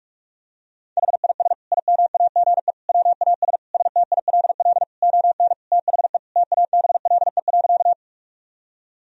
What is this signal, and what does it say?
Morse code, heard as a keyed tone, about 41 wpm, 700 Hz. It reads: HIS IMAGE WAS STILL ON THE TABLE.
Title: Sentences from Top 1000 Words - Review Part 4 - 45wpm